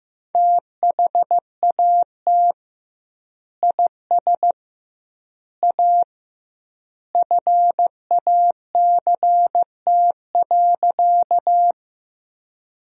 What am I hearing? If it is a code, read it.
Morse code, heard as a keyed tone, about 15 wpm, 700 Hz. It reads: THAT IS A FACT.
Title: 2x Word-Spacing - Sentences from Top 500 Words - 15wpm